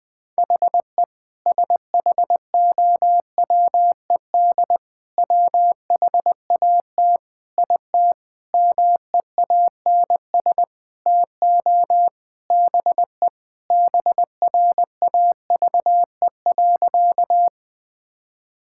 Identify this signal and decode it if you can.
Morse code, heard as a keyed tone, about 21 wpm, 700 Hz. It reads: HE SHOWED WHAT IT MEANS TO BE BRAVE.
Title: Sentences with 5-Word Phrases - 20wpm